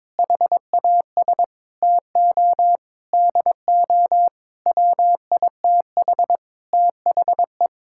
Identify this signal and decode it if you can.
Morse code, heard as a keyed tone, about 21 wpm, 700 Hz. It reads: HAS TO DO WITH THE
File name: Sentences with 9-Word Phrases - 22wpm